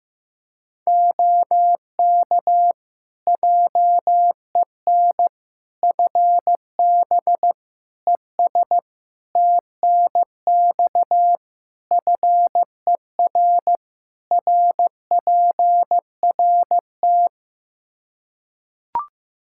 Morse code, heard as a keyed tone, about 15 wpm, 700 Hz. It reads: OK JEN FB ES TNX FER RPRT
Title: Practice QSO - Exchange 2 - Info - 15wpm